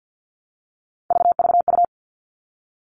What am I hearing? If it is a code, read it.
Morse code, heard as a keyed tone, about 50 wpm, 700 Hz. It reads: VVU
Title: All Cut Numbers - 3-Digit 50wpm